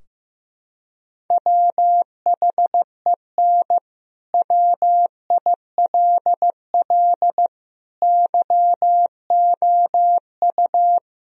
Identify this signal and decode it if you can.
Morse code, heard as a keyed tone, about 15 wpm, 700 Hz. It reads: WHEN WILL YOU
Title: Sentences from Top 1000 Words - Review Part 4 - 15wpm